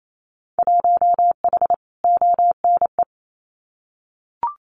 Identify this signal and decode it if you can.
Morse code, heard as a keyed tone, about 28 wpm, 700 Hz. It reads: 1H ONE
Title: Winter Field Day Exchanges 28wpm